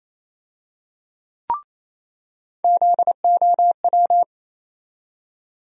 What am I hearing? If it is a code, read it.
Morse code, heard as a keyed tone, about 28 wpm, 700 Hz. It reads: ZOW
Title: USA Call Sign Postfixes - 3 Letters 28wpm